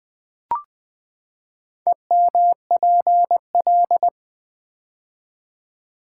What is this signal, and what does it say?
Morse code, heard as a keyed tone, about 20 wpm, 700 Hz. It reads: EMPL
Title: Build A Word - 4 Letters At A Time - 10-Letter Words - 20wpm